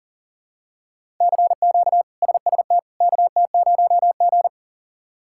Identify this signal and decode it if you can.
Morse code, heard as a keyed tone, about 40 wpm, 700 Hz. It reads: CQ SST KT0G
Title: K1USN SST QSO Practice - 40wpm